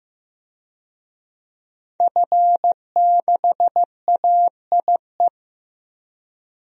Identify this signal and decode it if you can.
Morse code, heard as a keyed tone, about 15 wpm, 700 Hz. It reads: F6AIE
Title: DXCC-100 Call Signs - 5 Characters - Repeated 3x - 15wpm